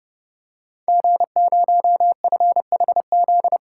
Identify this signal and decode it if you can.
Morse code, heard as a keyed tone, about 30 wpm, 700 Hz. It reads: G0FHZ
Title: Code Groups - 5 Letter:Numbers - 30wpm